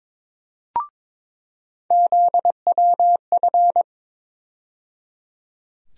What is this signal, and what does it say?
Morse code, heard as a keyed tone, about 22 wpm, 700 Hz. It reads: ZWF
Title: USA Call Sign Postfixes - 3 Letters 22wpm